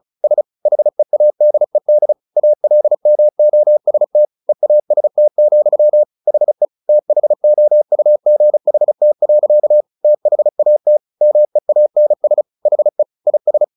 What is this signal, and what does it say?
Morse code, heard as a keyed tone, about 34 wpm, 600 Hz. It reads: S HEADED ALMOST EAST, HE THOUGHT. THAT MEANS HE IS
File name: the old man and the sea 2 35wpm